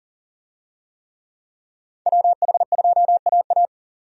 Morse code, heard as a keyed tone, about 40 wpm, 700 Hz. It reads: WH2AA
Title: USA Call Signs - 5 Characters - 40wpm